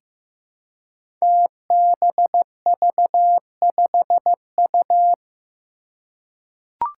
Morse code, heard as a keyed tone, about 15 wpm, 700 Hz. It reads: TBV5U
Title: Five Challenging Letter-Number Combinations 15wpm